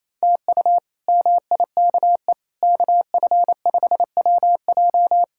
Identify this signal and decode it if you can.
Morse code, heard as a keyed tone, about 28 wpm, 700 Hz. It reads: TU MIKE KF5WJ
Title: K1USN SST QSO Practice - 28wpm